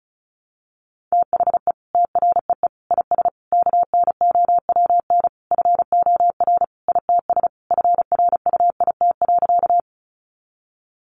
Morse code, heard as a keyed tone, about 34 wpm, 700 Hz. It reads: THE TREE IS KNOWN FOR ITS FRUIT.
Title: Sentences from Top 1300 Words 35wpm